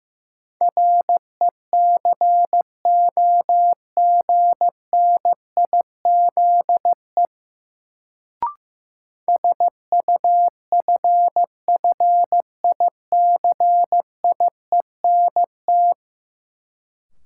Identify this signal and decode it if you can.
Morse code, heard as a keyed tone, about 15 wpm, 700 Hz. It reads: RECOGNIZE SUFFICIENT
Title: Sentences from Top 1400 Words - New Word Drill Alt - 15wpm